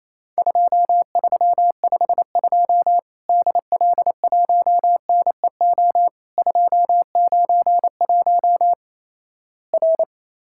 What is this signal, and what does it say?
Morse code, heard as a keyed tone, about 28 wpm, 700 Hz. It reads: 2352 DL1NEO 291 R
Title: Worked All Europe DX Contest - QTC Exchange - 28wpm